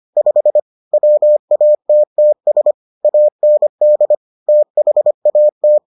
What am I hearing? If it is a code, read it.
Morse code, heard as a keyed tone, about 25 wpm, 600 Hz. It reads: 5 WATTS AND THAT